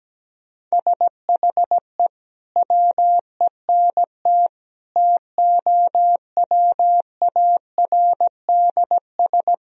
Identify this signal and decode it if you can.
Morse code, heard as a keyed tone, about 17 wpm, 700 Hz. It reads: SHE WENT TOWARDS
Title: Sentences from Top 1000 Words - Review Part 2 - 17wpm